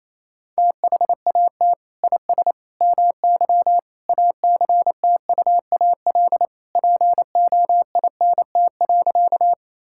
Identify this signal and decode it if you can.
Morse code, heard as a keyed tone, about 28 wpm, 700 Hz. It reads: THAT IS MY ACTUAL POINT.
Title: Sentences from Top 1000 Words - Review Part 5 - 28wpm